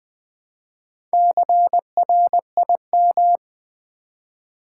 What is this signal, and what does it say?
Morse code, heard as a keyed tone, about 21 wpm, 700 Hz. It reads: CRIM
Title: Build A Word - 1 Letter At A Time - 5-Letter Words - 20wpm